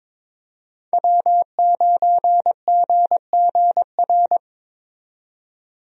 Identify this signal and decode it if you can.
Morse code, heard as a keyed tone, about 22 wpm, 700 Hz. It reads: W9GGR
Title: USA Call Signs - 5 Characters - 22wpm